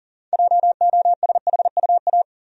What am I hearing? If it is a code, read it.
Morse code, heard as a keyed tone, about 40 wpm, 700 Hz. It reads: JOSHUA
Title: USA Common Names - 40wpm